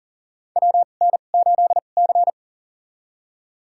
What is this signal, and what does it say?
Morse code, heard as a keyed tone, about 40 wpm, spaced as for 29 wpm, 700 Hz. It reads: WN8C
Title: USA Call Signs - 4 Characters - 2x Character Spacing - 40wpm